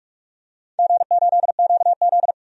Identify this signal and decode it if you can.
Morse code, heard as a keyed tone, about 44 wpm, 700 Hz. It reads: G8QZ
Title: DXCC-100 Call Signs - 4 Characters - 45wpm